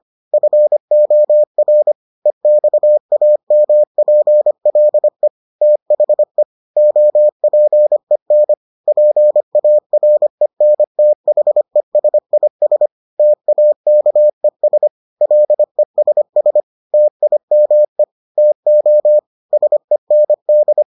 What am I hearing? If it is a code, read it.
Morse code, heard as a keyed tone, about 25 wpm, 600 Hz. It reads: FOR EXAMPLE THE OPEN PARENTHESIS TAKES LESS TIME TO SEND